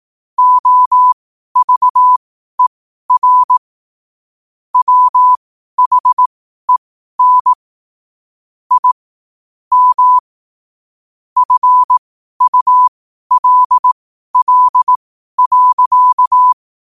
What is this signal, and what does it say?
Morse code, heard as a keyed tone, about 18 wpm, spaced as for 12 wpm, 1000 Hz. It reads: OVER WHEN I M FULL.
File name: qotdfiles / 12wpm